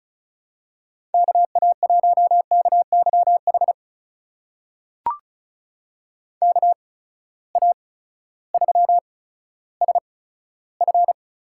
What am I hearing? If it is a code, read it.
Morse code, heard as a keyed tone, about 34 wpm, spaced as for 16 wpm, 700 Hz. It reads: KA1KYH KA3SF
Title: USA Call Signs - 8x Character Spacing - 35wpm